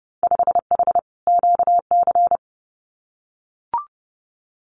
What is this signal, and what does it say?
Morse code, heard as a keyed tone, about 30 wpm, 700 Hz. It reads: 5H QC
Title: Winter Field Day Exchanges Encore 30wpm